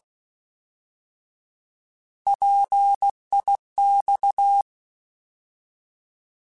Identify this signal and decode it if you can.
Morse code, heard as a keyed tone, about 16 wpm, 800 Hz. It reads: PIX
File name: Alphabetical / Alphabetical mp3s / Ps @15